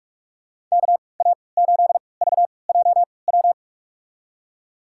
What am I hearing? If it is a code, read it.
Morse code, heard as a keyed tone, about 47 wpm, spaced as for 26 wpm, 700 Hz. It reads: KA8VJW
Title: USA Call Signs - 3x Character Spacing - 45wpm